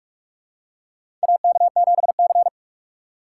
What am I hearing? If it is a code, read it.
Morse code, heard as a keyed tone, about 46 wpm, 700 Hz. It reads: AK7C
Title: USA Call Signs - 4 Characters - 4x Character Spacing - 45wpm